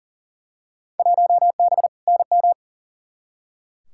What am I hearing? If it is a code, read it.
Morse code, heard as a keyed tone, about 40 wpm, 700 Hz. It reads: 1B NM
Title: Field Day Exchanges Encore - 40wpm